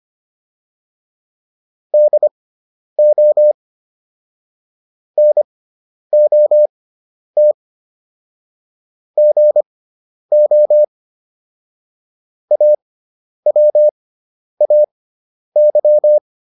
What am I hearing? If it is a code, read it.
Morse code, heard as a keyed tone, about 25 wpm, spaced as for 10 wpm, 600 Hz. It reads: DO NOT GO AWAY
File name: WD304 10